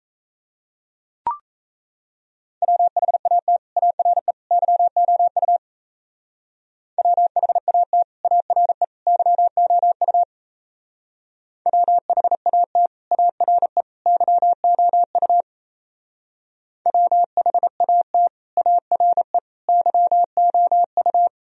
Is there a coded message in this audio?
Yes: Morse code from 2.6 to 21.3 s, heard as a keyed tone, about 37 wpm, 700 Hz. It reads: WHAT ARE YOU WHAT ARE YOU WHAT ARE YOU WHAT ARE YOU